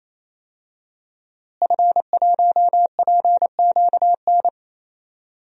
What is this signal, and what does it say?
Morse code, heard as a keyed tone, about 28 wpm, 700 Hz. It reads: F1PQN
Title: DXCC-100 Call Signs - 5 Characters - Repeated 3x - 28wpm